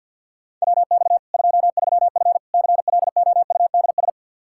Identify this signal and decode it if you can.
Morse code, heard as a keyed tone, about 47 wpm, 700 Hz. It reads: WX 23F CLOUDS
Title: Practice QSO - Exchange 2 - Info - 50wpm